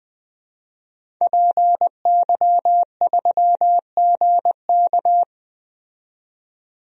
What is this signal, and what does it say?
Morse code, heard as a keyed tone, about 20 wpm, 700 Hz. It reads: PY3GK
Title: DXCC-100 Call Signs - 5 Characters - 20wpm